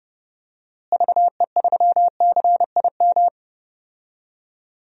Morse code, heard as a keyed tone, about 30 wpm, 700 Hz. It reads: VE3CIM
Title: DXCC-100 Call Signs - 6 Characters - Repeated 2x - 30wpm